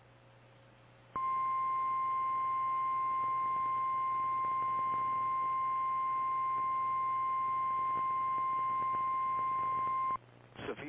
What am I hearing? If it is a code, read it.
Emergency Alert System (SAME) tones (the 1050 Hz attention tone), with speech. It is heard from 1.1 to 10.2 s.
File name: eas / KKPRKXPN / KKPRKXPN